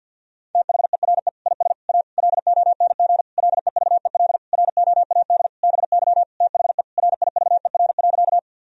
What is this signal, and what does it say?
Morse code, heard as a keyed tone, about 50 wpm, 700 Hz. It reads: THERE IS A LONG LEVEL ROAD BY THE RIVER.
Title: Sentences from Top 1000 Words - Review Part 4 - 50wpm